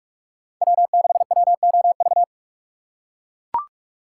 Morse code, heard as a keyed tone, about 41 wpm, 700 Hz. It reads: W6WOV
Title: USA Call Signs - 5 Characters - 3x Character Spacing - 45wpm